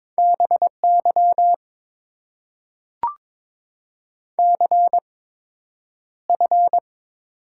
Morse code, heard as a keyed tone, about 22 wpm, spaced as for 10 wpm, 690 Hz. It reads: BY CF
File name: Top 90 QSO Elements - 8x Character Spacing - 22wpm